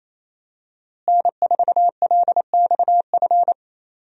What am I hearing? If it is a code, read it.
Morse code, heard as a keyed tone, about 27 wpm, 700 Hz. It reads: N4LXF
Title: USA Call Signs - 5 Characters - Repeated 2x - 28wpm